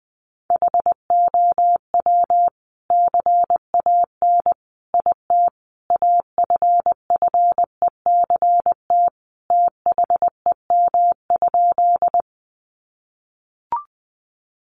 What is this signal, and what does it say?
Morse code, heard as a keyed tone, about 20 wpm, 700 Hz. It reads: HOW CAN IT AFFECT THEM?
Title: Sentences from Top 900 Words 20wpm